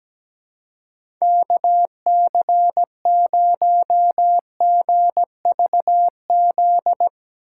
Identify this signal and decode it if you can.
Morse code, heard as a keyed tone, about 17 wpm, 700 Hz. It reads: KC0GVZ
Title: USA Call Signs - 6 Characters - 4x Character Spacing - 17wpm